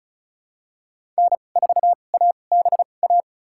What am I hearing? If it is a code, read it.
Morse code, heard as a keyed tone, about 34 wpm, spaced as for 25 wpm, 700 Hz. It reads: N4ABA
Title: USA Call Signs - 5 Characters - 2x Character Spacing - 35wpm